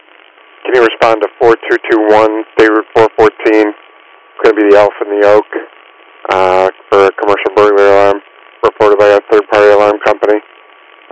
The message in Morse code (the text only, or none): none